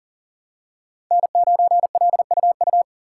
Morse code, heard as a keyed tone, about 40 wpm, 700 Hz. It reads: N9LUU
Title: USA Call Signs - 5 Characters - 2x Character Spacing - 40wpm